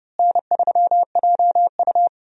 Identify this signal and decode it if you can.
Morse code, heard as a keyed tone, about 30 wpm, 700 Hz. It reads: N3JU
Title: USA Call Signs - 4 Characters - Repeated 3x - 30wpm